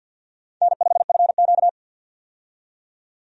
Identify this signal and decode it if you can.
Morse code, heard as a keyed tone, about 49 wpm, 700 Hz. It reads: N5FQ
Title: USA Call Signs - 4 Characters - 50wpm